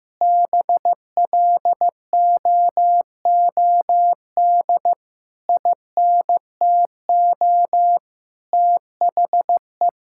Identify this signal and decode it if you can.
Morse code, heard as a keyed tone, about 15 wpm, 700 Hz. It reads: BLOOD INTO THE
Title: Sentences from Top 1600 Words 15wpm